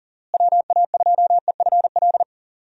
Morse code, heard as a keyed tone, about 40 wpm, 700 Hz. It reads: WA2EFL
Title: USA Call Signs - 6 Characters - Repeated 3x - 40wpm